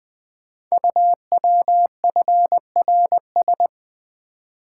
Morse code, heard as a keyed tone, about 20 wpm, 700 Hz. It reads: UWFRS
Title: Code Groups - 5 Letters - 20wpm